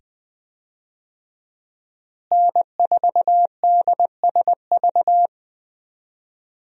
Morse code, heard as a keyed tone, about 20 wpm, 700 Hz. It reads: N4DSV
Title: USA Call Signs - 5 Characters - Repeated 2x - 20wpm